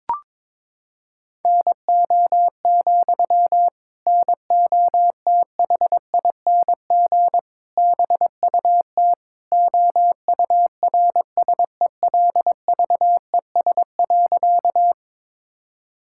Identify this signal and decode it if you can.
Morse code, heard as a keyed tone, about 22 wpm, 700 Hz. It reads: NO, NOTHING BUT OURSELVES.
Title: Sentences from Top 1600 Words 22wpm